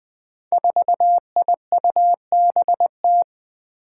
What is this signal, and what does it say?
Morse code, heard as a keyed tone, about 20 wpm, 700 Hz. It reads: 4IUBT
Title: Five Challenging Letter-Number Combinations 20wpm